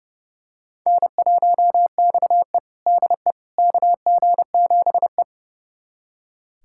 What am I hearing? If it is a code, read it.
Morse code, heard as a keyed tone, about 30 wpm, 700 Hz. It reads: N1XE DE KG7E
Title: USA Callsign DE Callsign - 4 Characters - 30wpm